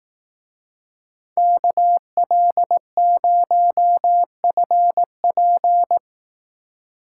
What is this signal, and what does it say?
Morse code, heard as a keyed tone, about 18 wpm, 700 Hz. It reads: KL0FP
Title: Speed Racing - USA 2x2 Call Signs 15wpm